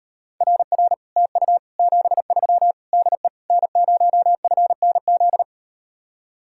Morse code, heard as a keyed tone, about 38 wpm, 700 Hz. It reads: RR TU 73 DE N0FNZ